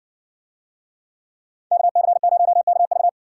Morse code, heard as a keyed tone, about 60 wpm, 700 Hz. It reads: KC0KV